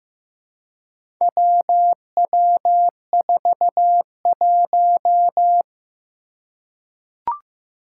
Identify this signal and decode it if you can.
Morse code, heard as a keyed tone, about 15 wpm, 700 Hz. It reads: WW41
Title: Four Challenging Letter-Number Combinations 15wpm